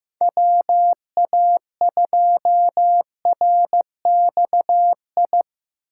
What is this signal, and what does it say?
Morse code, heard as a keyed tone, about 15 wpm, 700 Hz. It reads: WA2RXI
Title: USA Call Signs - 6 Characters - 6x Character Spacing - 15wpm